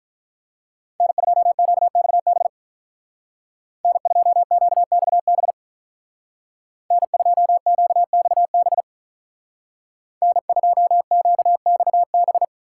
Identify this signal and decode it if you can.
Morse code, heard as a keyed tone, about 43 wpm, 700 Hz. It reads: N2QXB N2QXB N2QXB N2QXB
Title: Speed Racing - USA 1x3 Call Signs 35wpm